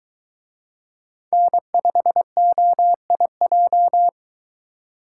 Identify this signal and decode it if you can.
Morse code, heard as a keyed tone, about 23 wpm, 700 Hz. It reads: N5OIJ